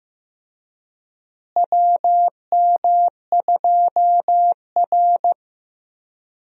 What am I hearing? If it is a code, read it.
Morse code, heard as a keyed tone, about 15 wpm, 700 Hz. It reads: WM2R
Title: USA Call Signs - 4 Characters - Repeated 3x - 15wpm